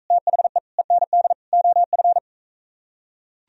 Morse code, heard as a keyed tone, about 41 wpm, 700 Hz. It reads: THE END OF